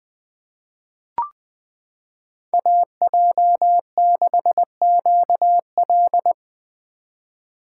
Morse code, heard as a keyed tone, about 20 wpm, 700 Hz. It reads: AJ6QL
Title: K1USN SST QSO Practice - 20wpm